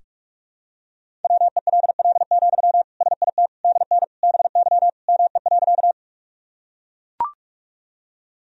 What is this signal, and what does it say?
Morse code, heard as a keyed tone, about 42 wpm, 700 Hz. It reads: WELL, SIT DN BY ME.
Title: Sentences from Top 500 Words and Abbreviations 45wpm